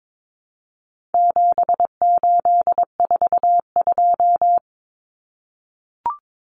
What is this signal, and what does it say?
Morse code, heard as a keyed tone, about 22 wpm, 700 Hz. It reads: 7842
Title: Numbers - 4-Digit 22wpm